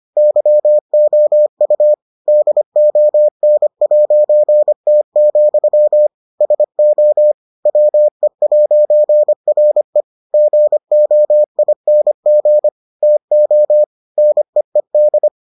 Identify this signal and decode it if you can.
Morse code, heard as a keyed tone, about 25 wpm, 600 Hz. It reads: YOU DON'T, SO WE'RE GOING TO NEED